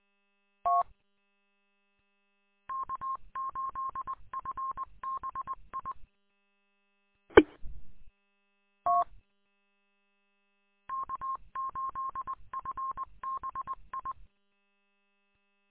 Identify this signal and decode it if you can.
Morse code, heard as a keyed tone, about 20 wpm, 1070 Hz. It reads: T K8FBI E T K8FBI
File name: k8fbi-43732-combined